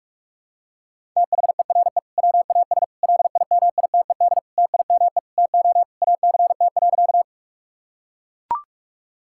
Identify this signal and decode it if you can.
Morse code, heard as a keyed tone, about 41 wpm, 700 Hz. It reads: THERE WAS LIMITED TIME TO ACT.
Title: Sentences from Top 1000 Words - Review Part 6 - 45wpm